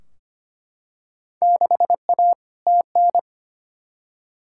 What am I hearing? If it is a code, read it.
Morse code, heard as a keyed tone, about 25 wpm, 700 Hz. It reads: 6A TN
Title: Field Day Exchanges Encore - 25wpm